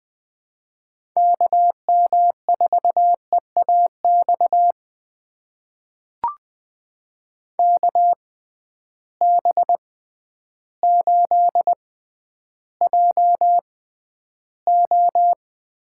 Morse code, heard as a keyed tone, about 20 wpm, spaced as for 11 wpm, 700 Hz. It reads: KM4EAX KB8JO
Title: USA Call Signs - 6 Characters - 6x Character Spacing - 20wpm